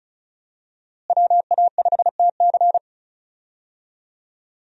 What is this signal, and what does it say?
Morse code, heard as a keyed tone, about 34 wpm, 700 Hz. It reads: WA5TC
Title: USA Call Signs - 5 Characters - Repeated 3x - 35wpm